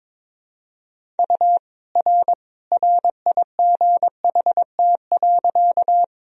Morse code, heard as a keyed tone, about 21 wpm, 700 Hz. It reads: U R RIGHT.
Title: Sentences from Top 200 Words and Abbreviations 22wpm